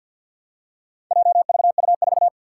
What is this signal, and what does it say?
Morse code, heard as a keyed tone, about 50 wpm, 700 Hz. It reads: JVU4
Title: Four Challenging Letter-Number Combinations 50wpm